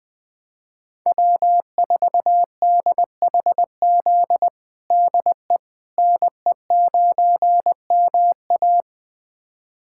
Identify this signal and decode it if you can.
Morse code, heard as a keyed tone, about 20 wpm, 700 Hz. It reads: W4DHZ DE NE9MA
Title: USA Callsign DE Callsign - 5 Characters - 20wpm